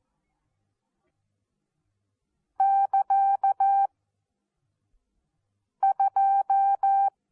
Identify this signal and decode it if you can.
Morse code, heard as a keyed tone, about 14 wpm, 780 Hz. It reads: <KA> 2
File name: Onlinelehrgang / morsen / DJ4UF Morsen14